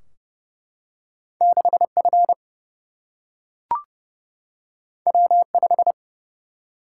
Morse code, heard as a keyed tone, about 30 wpm, 700 Hz. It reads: 6 F W 5
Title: Single-Letter Single-Number 30wpm